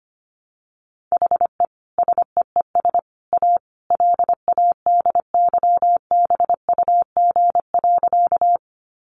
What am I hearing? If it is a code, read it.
Morse code, heard as a keyed tone, about 25 wpm, 700 Hz. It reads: HE SEES A LADYBUG.
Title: Additional Easy Phrases 25wpm